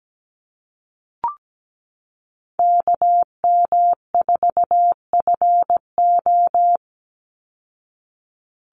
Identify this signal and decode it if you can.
Morse code, heard as a keyed tone, about 17 wpm, 700 Hz. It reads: KM4FO
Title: Call Signs Repeated 2x - 17wpm